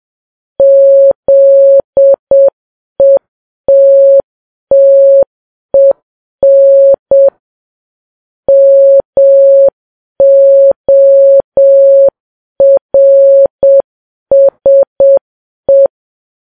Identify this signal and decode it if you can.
Morse code, heard as a keyed tone, about 7 wpm, 550 Hz. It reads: ZETTEN MORSE
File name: Files Web tls / Morse on9fv / morse7kps